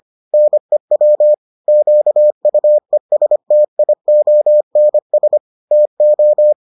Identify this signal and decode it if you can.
Morse code, heard as a keyed tone, about 25 wpm, 600 Hz. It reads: NEW QUESTIONS TO